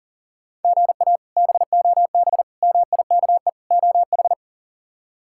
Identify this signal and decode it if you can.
Morse code, heard as a keyed tone, about 40 wpm, 700 Hz. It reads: GA BOB MIKE OH
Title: K1USN SST QSO Practice - 40wpm